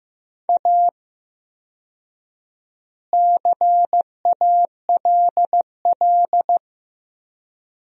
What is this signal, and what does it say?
Morse code, heard as a keyed tone, about 15 wpm, 700 Hz. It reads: A CALL
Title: 4x Word-Spacing - Sentences from Top 300 Words - 15wpm